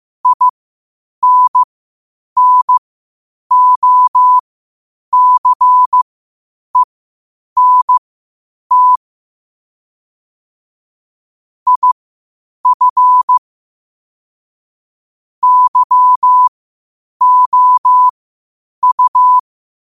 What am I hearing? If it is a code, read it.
Morse code, heard as a keyed tone, about 15 wpm, spaced as for 8 wpm, 1000 Hz. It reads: INNOCENT IF YOU
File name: qotdfiles / 8wpm